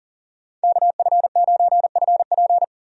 Morse code, heard as a keyed tone, about 40 wpm, 700 Hz. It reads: KF9FP
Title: USA Call Signs - 5 Characters - 40wpm